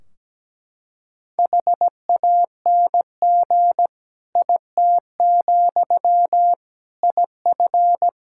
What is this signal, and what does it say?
Morse code, heard as a keyed tone, about 17 wpm, 700 Hz. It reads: HANG IT, IF
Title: Sentences from Top 1600 Words 17wpm